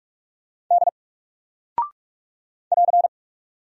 Morse code, heard as a keyed tone, about 44 wpm, 700 Hz. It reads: D <AR>
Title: Single Character (Letter:Number:Punctuation:Prosign) Alt 45wpm